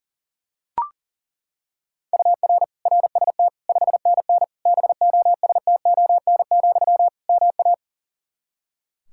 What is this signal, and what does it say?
Morse code, heard as a keyed tone, about 40 wpm, 700 Hz. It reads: UR RST 5NN BOSTON, MA
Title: Practice QSO - Exchange 1 - Beginning - 40wpm